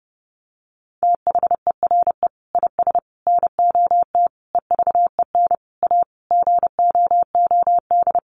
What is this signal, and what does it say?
Morse code, heard as a keyed tone, about 30 wpm, 700 Hz. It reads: THERE IS NOT EVEN A GOOD
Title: Sentences from Top 1500 Words 30wpm